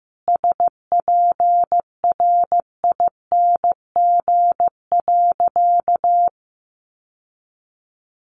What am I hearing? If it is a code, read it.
Morse code, heard as a keyed tone, about 15 wpm, 700 Hz. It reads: SPRING.
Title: Sentences from Top 1200 Words 15wpm